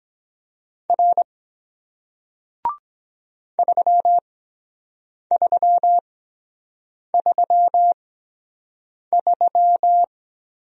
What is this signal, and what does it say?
Morse code, heard as a keyed tone, about 23 wpm, spaced as for 7 wpm, 700 Hz. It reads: R 3333